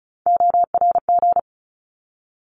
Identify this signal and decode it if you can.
Morse code, heard as a keyed tone, about 34 wpm, 700 Hz. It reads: ORG
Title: ARRL Sections 35wpm